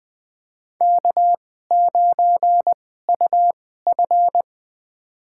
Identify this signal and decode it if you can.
Morse code, heard as a keyed tone, about 21 wpm, spaced as for 15 wpm, 700 Hz. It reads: K9UF
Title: USA Call Signs - 2x Character Spacing - 20wpm